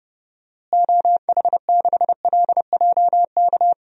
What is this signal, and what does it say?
Morse code, heard as a keyed tone, about 30 wpm, 700 Hz. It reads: OH6LJK
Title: DXCC-100 Call Signs - 6 Characters - Repeated 2x - 30wpm